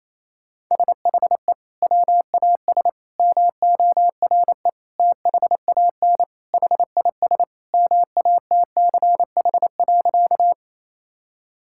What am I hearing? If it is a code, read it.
Morse code, heard as a keyed tone, about 28 wpm, 700 Hz. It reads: SHE WAS MORE THAN HIS MATCH.
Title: Sentences from Top 1600 Words 28wpm